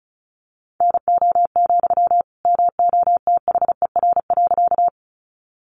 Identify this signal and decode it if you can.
Morse code, heard as a keyed tone, about 34 wpm, 700 Hz. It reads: NO, MOTHER.
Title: Sentences from Top 200 Words 35wpm